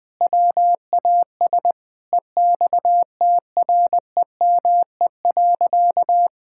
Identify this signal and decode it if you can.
Morse code, heard as a keyed tone, about 20 wpm, 700 Hz. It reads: WAS EXTREME.
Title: Sentences from Top 1600 Words 20wpm